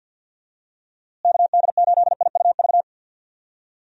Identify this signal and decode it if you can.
Morse code, heard as a keyed tone, about 51 wpm, 700 Hz. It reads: KD8IUV
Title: USA Call Signs - 6 Characters - Repeated 2x - 50wpm